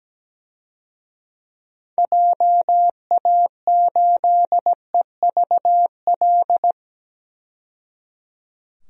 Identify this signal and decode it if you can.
Morse code, heard as a keyed tone, about 17 wpm, 700 Hz. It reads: JA8EVL
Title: DXCC-100 Call Signs - 6 Characters - 17wpm